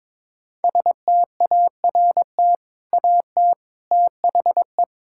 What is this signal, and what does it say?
Morse code, heard as a keyed tone, about 22 wpm, 700 Hz. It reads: START AT THE
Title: Sentences from Top 1000 Words - Review Part 6 - 22wpm